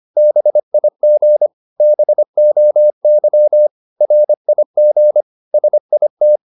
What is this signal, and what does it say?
Morse code, heard as a keyed tone, about 25 wpm, 600 Hz. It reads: BIG BOY RIG SIT